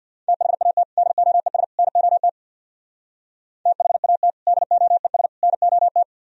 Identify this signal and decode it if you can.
Morse code, heard as a keyed tone, about 63 wpm, 700 Hz. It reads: THAT DOES NOT THAT DOES NOT